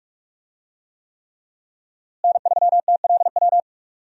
Morse code, heard as a keyed tone, about 42 wpm, 700 Hz. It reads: N3TLW
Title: USA Call Signs - 5 Characters - Repeated 3x - 45wpm